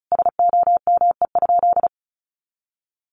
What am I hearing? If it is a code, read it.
Morse code, heard as a keyed tone, about 34 wpm, 700 Hz. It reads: SOME?
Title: Sentences from Top 100 Words 35wpm